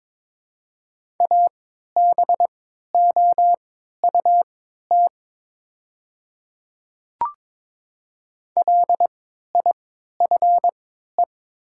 Morse code, heard as a keyed tone, about 22 wpm, spaced as for 13 wpm, 700 Hz. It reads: ABOUT LIFE
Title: 3x Character Spacing - Top 500 Words - 22wpm